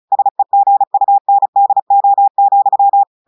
Morse code, heard as a keyed tone, about 36 wpm, 800 Hz. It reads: SEGUNDO,
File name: audios / portugais / portugais texte en clair 1 vitesse 35 mots minutes